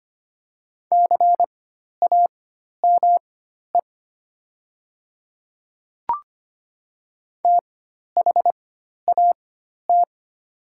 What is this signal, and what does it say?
Morse code, heard as a keyed tone, about 25 wpm, spaced as for 12 wpm, 700 Hz. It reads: CAME THAT
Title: 4x Character Spacing - Top 300 Words - 25wpm